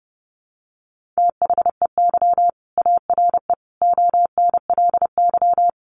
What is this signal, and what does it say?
Morse code, heard as a keyed tone, about 30 wpm, 700 Hz. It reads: THEY ARE ONLY
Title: Sentences from Top 1400 Words 30wpm